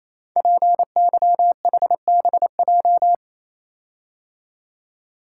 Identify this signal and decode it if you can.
Morse code, heard as a keyed tone, about 29 wpm, 700 Hz. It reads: PYHBJ
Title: Code Groups - 5 Letters - 28wpm